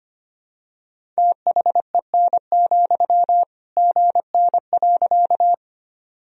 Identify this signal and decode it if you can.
Morse code, heard as a keyed tone, about 25 wpm, 700 Hz. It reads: THEN, GN.
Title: Sentences from Top 300 Words and Abbreviations 25wpm